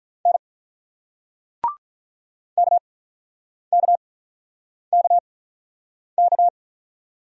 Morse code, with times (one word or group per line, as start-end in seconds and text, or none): 0.3-0.4 N
2.6-6.5 KKKK